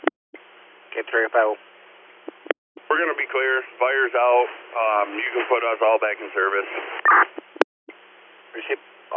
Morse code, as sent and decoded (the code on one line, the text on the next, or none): none